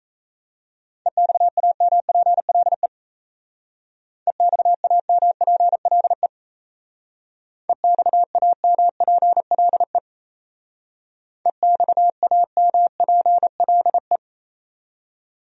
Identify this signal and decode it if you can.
Morse code, heard as a keyed tone, about 34 wpm, 700 Hz. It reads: EXAMPLE EXAMPLE EXAMPLE EXAMPLE
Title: Speed-Racing Top 500 Words 28wpm